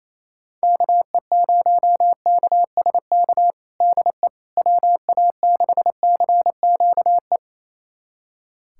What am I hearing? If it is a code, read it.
Morse code, heard as a keyed tone, about 28 wpm, 700 Hz. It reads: KE0KSK DE WA6CQE
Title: USA Callsign DE Callsign 28wpm